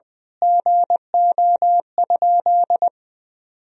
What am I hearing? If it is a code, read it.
Morse code, heard as a keyed tone, about 21 wpm, 700 Hz. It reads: GO?
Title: Sentences from Top 1000 Words - Review Part 4 - 20wpm